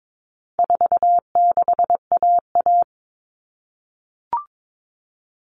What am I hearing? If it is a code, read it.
Morse code, heard as a keyed tone, about 22 wpm, 700 Hz. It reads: 46AA
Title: All Cut Numbers - 4-Digit 22wpm